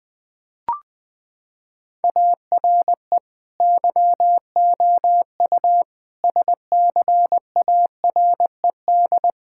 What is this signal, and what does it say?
Morse code, heard as a keyed tone, about 20 wpm, 700 Hz. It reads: ARE YOU SCARED
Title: Sentences from Top 1000 Words - Review Part 5 - 20wpm